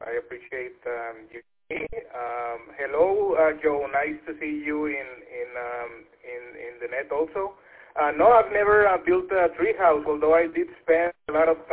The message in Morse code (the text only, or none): none